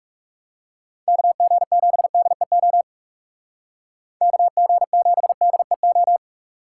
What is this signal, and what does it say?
Morse code, heard as a keyed tone, about 37 wpm, 700 Hz. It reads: KG7DEO KG7DEO
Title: Speed Racing - USA 2x3 Call Signs 30wpm